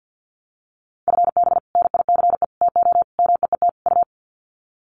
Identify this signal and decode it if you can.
Morse code, heard as a keyed tone, about 50 wpm, 700 Hz. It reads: FB NICE TO MEET U
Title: Common QSO Phrases 50wpm